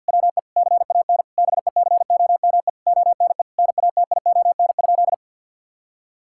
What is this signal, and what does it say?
Morse code, heard as a keyed tone, about 52 wpm, 700 Hz. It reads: WE CAN BECOME ONE NATION?
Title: Sentences from Top 1000 Words - Review Part 6 - 50wpm